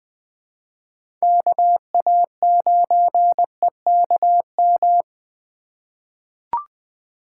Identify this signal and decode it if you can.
Morse code, heard as a keyed tone, about 21 wpm, 700 Hz. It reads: KA9EKM
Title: USA Call Signs - 2x Character Spacing - 20wpm